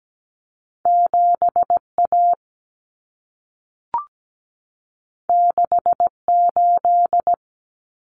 Morse code, heard as a keyed tone, about 17 wpm, 700 Hz. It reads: 7A 68
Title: Common Cut Numbers - 2-Digit 17wpm